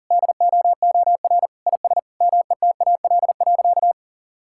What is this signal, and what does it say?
Morse code, heard as a keyed tone, about 40 wpm, 700 Hz. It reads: DOOR IS METAL.
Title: Sentences from Top 1600 Words 40wpm